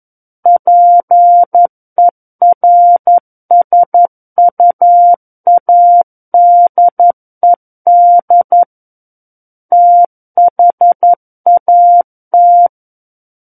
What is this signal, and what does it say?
Morse code, heard as a keyed tone, about 11 wpm, 700 Hz. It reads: PERSUADED THAT
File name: 11 WPM